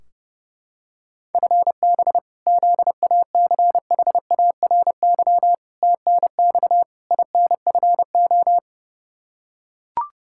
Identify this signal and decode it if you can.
Morse code, heard as a keyed tone, about 30 wpm, 700 Hz. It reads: FB ZACHARY TNX INFO
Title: Practice QSO - Exchange 3 - More Info - 30wpm